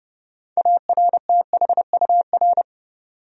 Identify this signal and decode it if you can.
Morse code, heard as a keyed tone, about 30 wpm, 700 Hz. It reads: ARTHUR
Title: USA Common Names - 30wpm